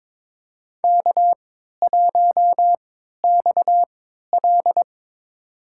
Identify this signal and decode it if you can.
Morse code, heard as a keyed tone, about 22 wpm, spaced as for 12 wpm, 700 Hz. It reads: K1XL
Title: USA Call Signs - 4 Characters - 3x Character Spacing - 22wpm